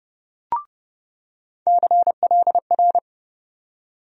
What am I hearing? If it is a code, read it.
Morse code, heard as a keyed tone, about 30 wpm, 700 Hz. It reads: CLR
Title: USA Call Sign Postfixes - 3 Letters 30wpm